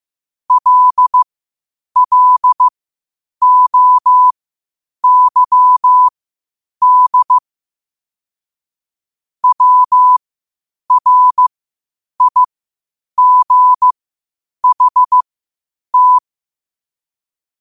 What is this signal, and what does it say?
Morse code, heard as a keyed tone, about 15 wpm, spaced as for 8 wpm, 1000 Hz. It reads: LLOYD WRIGHT
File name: qotdfiles / 8wpm